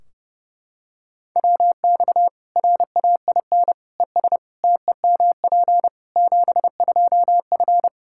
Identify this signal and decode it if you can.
Morse code, heard as a keyed tone, about 30 wpm, 700 Hz. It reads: WX RAIN ES TEMP 72F
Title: K1USN SST QSO Practice 30wpm